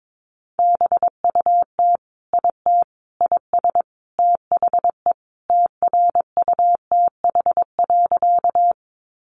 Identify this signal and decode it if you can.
Morse code, heard as a keyed tone, about 22 wpm, 700 Hz. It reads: BUT IT IS THE TRUTH.
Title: Sentences from Top 800 Words 22wpm